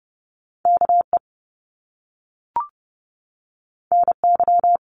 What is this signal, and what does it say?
Morse code, heard as a keyed tone, about 30 wpm, 700 Hz. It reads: KE NY
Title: Top Two Letter Combinations in Words 30wpm